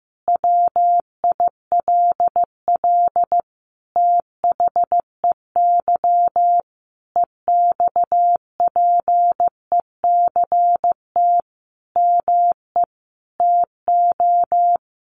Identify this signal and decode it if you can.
Morse code, heard as a keyed tone, about 15 wpm, 700 Hz. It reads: WILL THEY EXPECT ME TO
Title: Sentences from Top 1100 Words 15wpm